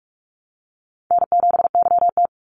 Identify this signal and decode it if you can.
Morse code, heard as a keyed tone, about 44 wpm, 700 Hz. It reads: N7YT
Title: Call Signs Repeated 2x - 45wpm